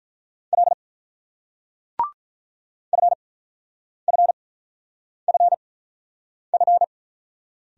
Morse code, heard as a keyed tone, about 46 wpm, spaced as for 9 wpm, 700 Hz. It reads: L FFFF